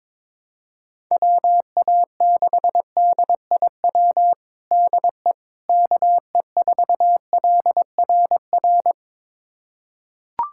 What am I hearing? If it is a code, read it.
Morse code, heard as a keyed tone, about 22 wpm, 700 Hz. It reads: WA6DIW DE KE4LRR
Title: USA Callsign DE Callsign 22wpm